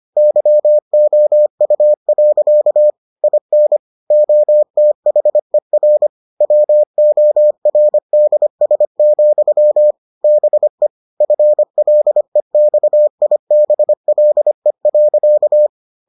Morse code, heard as a keyed tone, about 25 wpm, 600 Hz. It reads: YOU. IN OTHER WORDS, BE FLEXIBLE.